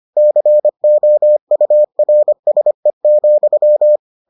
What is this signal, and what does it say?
Morse code, heard as a keyed tone, about 25 wpm, 600 Hz. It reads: COURSE,